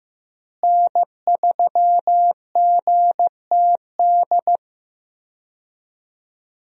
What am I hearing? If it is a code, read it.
Morse code, heard as a keyed tone, about 15 wpm, 700 Hz. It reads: N3GTD
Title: USA Call Signs - 5 Characters - Repeated 2x - 15wpm